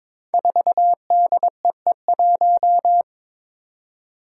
Morse code, heard as a keyed tone, about 22 wpm, 700 Hz. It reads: 4DEE1
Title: Five Challenging Letter-Number Combinations 22wpm